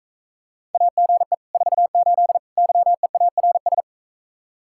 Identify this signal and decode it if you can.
Morse code, heard as a keyed tone, about 44 wpm, 700 Hz. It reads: AGE 48 YEARS